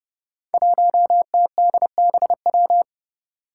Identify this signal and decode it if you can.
Morse code, heard as a keyed tone, about 30 wpm, 700 Hz. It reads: 1TDBW
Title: Five Challenging Letter-Number Combinations 30wpm